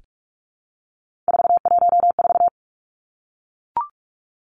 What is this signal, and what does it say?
Morse code, heard as a keyed tone, about 44 wpm, 700 Hz. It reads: 414
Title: Numbers - 3-Digit 45wpm